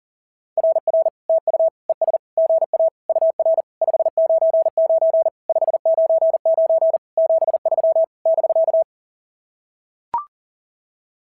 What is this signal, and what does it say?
Morse code, heard as a keyed tone, about 40 wpm, 690 Hz. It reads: RR TU ES GA UR 599 599 73 <BK>
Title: SOTA QSO Practice 40wpm